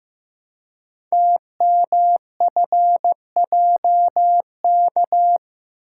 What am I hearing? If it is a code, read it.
Morse code, heard as a keyed tone, about 15 wpm, 700 Hz. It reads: TMFJK
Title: Code Groups - 5 Letter:Numbers - 15wpm